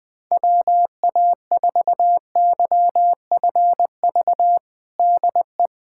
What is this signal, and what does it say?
Morse code, heard as a keyed tone, about 20 wpm, 700 Hz. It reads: WA4YFV DE
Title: USA Callsign DE Callsign 20wpm